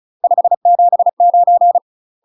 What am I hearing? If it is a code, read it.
Morse code, heard as a keyed tone, about 36 wpm, 700 Hz. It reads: 579
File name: UK MASTER QSOs TRAINING / 35WPM / UK MASTER QSOs 0002